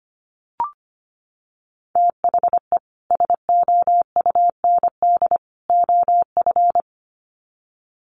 Morse code, heard as a keyed tone, about 25 wpm, 700 Hz. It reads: THE SOUND OF
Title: Sets of 3 Words 25wpm